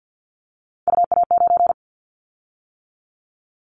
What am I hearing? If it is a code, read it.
Morse code, heard as a keyed tone, about 50 wpm, 690 Hz. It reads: UA9
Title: Callsign Prefixes for Top 100 DXCC - 50wpm